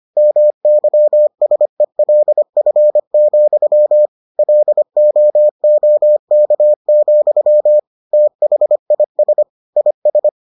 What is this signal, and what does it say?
Morse code, heard as a keyed tone, about 25 wpm, 600 Hz. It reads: MYSELF, LOOK, THIS IS